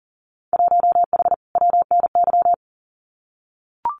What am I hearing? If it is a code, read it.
Morse code, heard as a keyed tone, about 42 wpm, 700 Hz. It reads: 1H WNY
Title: Winter Field Day Exchanges 40wpm